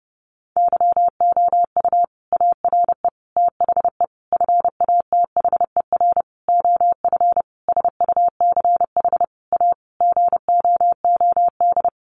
Morse code, heard as a keyed tone, about 30 wpm, 700 Hz. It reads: YOU ARE THE FATHER OF SUCH A GOOD
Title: Sentences from Top 1500 Words 30wpm